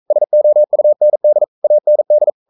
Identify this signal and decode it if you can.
Morse code, heard as a keyed tone, about 42 wpm, 600 Hz. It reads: SOUND AND
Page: Romanen Never af Ken Follet på engelsk.